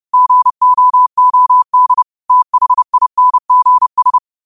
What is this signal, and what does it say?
Morse code, heard as a keyed tone, about 30 wpm, 1000 Hz. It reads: GOOD THINGS
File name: qotdfiles / 30wpm